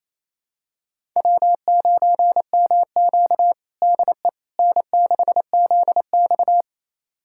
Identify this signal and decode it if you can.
Morse code, heard as a keyed tone, about 28 wpm, 700 Hz. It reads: W9MQ DE N6ZX
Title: USA Callsign DE Callsign - 4 Characters - 28wpm